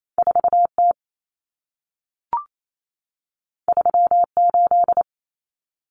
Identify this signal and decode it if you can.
Morse code, heard as a keyed tone, about 28 wpm, 700 Hz. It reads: 4T 38
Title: Common Cut Numbers - 2-Digit 28wpm